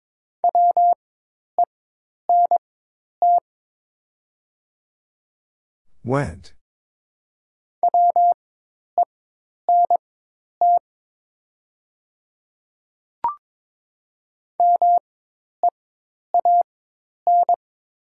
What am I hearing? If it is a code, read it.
Morse code, heard as a keyed tone, about 22 wpm, spaced as for 10 wpm, 700 Hz. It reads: WENT WENT MEAN
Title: 4x Character Spacing - Top 200 Words - 22wpm